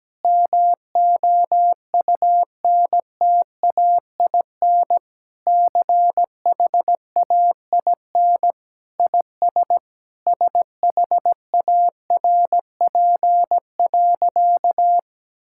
Morse code, heard as a keyed tone, about 17 wpm, 700 Hz. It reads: MOUNTAIN CHAIN IS SHARP.
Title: Sentences from Top 1600 Words 17wpm